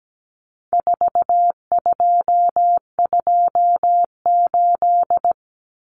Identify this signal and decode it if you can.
Morse code, heard as a keyed tone, about 17 wpm, 700 Hz. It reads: 4228
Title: Numbers - 4-Digit 17wpm